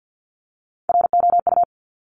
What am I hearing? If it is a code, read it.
Morse code, heard as a keyed tone, about 50 wpm, 690 Hz. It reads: ROU
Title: Top Three Letter Combinations in Words 50wpm